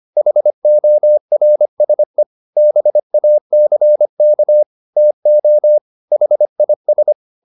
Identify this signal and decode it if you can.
Morse code, heard as a keyed tone, about 25 wpm, 600 Hz. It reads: HORSE BACK TO HIS